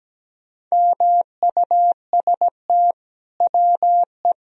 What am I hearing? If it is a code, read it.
Morse code, heard as a keyed tone, about 17 wpm, 700 Hz. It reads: MUST WE
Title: Sentences from Top 1000 Words - Review Part 1 - 17wpm